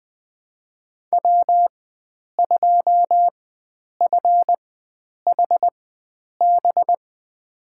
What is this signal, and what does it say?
Morse code, heard as a keyed tone, about 21 wpm, spaced as for 9 wpm, 700 Hz. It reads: W2FHB
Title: USA Call Signs - 5 Characters - 4x Character Spacing - 20wpm